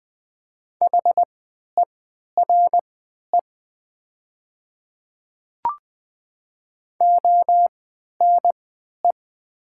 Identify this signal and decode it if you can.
Morse code, heard as a keyed tone, about 20 wpm, spaced as for 11 wpm, 700 Hz. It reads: HERE ONE
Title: 3x Character Spacing - Top 300 Words - 20wpm